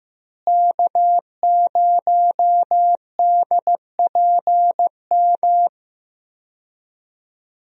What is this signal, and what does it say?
Morse code, heard as a keyed tone, about 15 wpm, 700 Hz. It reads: K0DPM
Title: USA Call Signs - 5 Characters - Repeated 2x - 15wpm